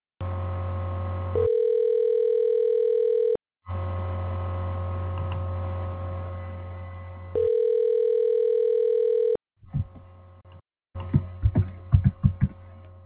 Ringback tone (North America), 440 + 480 Hz.
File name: recordings / REca286e21fbfd b